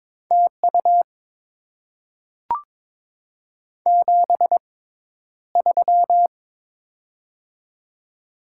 Morse code, heard as a keyed tone, about 22 wpm, spaced as for 11 wpm, 700 Hz. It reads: TU 73
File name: Top 20 QSO Elements - 6x Character Spacing - 22wpm